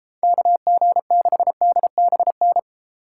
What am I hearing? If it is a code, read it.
Morse code, heard as a keyed tone, about 34 wpm, 700 Hz. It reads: KG6DBN